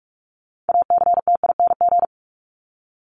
Morse code, heard as a keyed tone, about 44 wpm, 700 Hz. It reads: ACTING
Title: ING Suffix - Top 100 Words with Base Words 45wpm